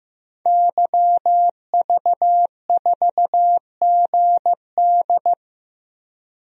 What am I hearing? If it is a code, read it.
Morse code, heard as a keyed tone, about 15 wpm, 700 Hz. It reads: YV4GD
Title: DXCC-100 Call Signs - 5 Characters - Repeated 2x - 15wpm